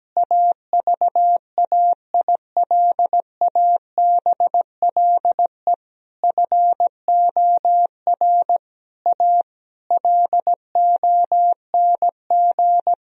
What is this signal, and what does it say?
Morse code, heard as a keyed tone, about 17 wpm, 700 Hz. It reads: AVAILABLE FOR A LONG
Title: Sentences from Top 1000 Words - Review Part 4 - 17wpm